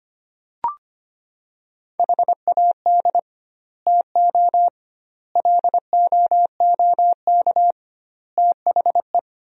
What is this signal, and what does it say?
Morse code, heard as a keyed tone, about 25 wpm, 700 Hz. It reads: HAD TO LOOK THE
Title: 2x Word-Spacing - Sentences from Top 100 Words - 25wpm